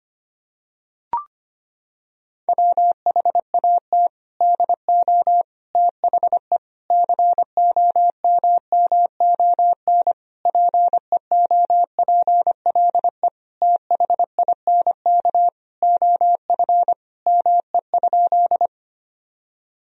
Morse code, heard as a keyed tone, about 25 wpm, 700 Hz. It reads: WHAT DO THE COMMON PEOPLE THINK OF ME?
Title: Sentences from Top 1000 Words - Review Part 3 - 25wpm